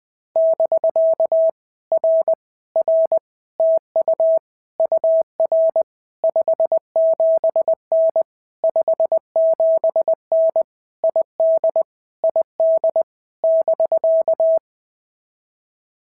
Morse code, heard as a keyed tone, about 20 wpm, 650 Hz. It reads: <BK> R R TU UR 57N 57N ID ID <BK>
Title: POTA QSO Practice - 20wpm